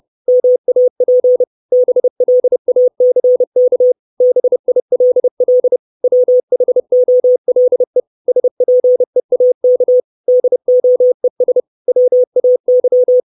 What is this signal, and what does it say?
Morse code, heard as a keyed tone, about 30 wpm, 500 Hz. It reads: MAP BLACK BILL WHOLE SPEAK DOES WAY
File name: 632 Words 30WPM